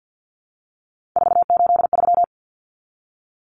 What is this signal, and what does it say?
Morse code, heard as a keyed tone, about 50 wpm, 700 Hz. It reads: 483
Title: Numbers - 3-Digit 50wpm